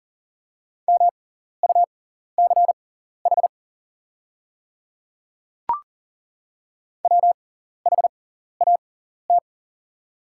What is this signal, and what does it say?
Morse code, heard as a keyed tone, about 40 wpm, spaced as for 14 wpm, 700 Hz. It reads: MUCH WHAT
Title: 6x Character Spacing - Top 300 Words - 40wpm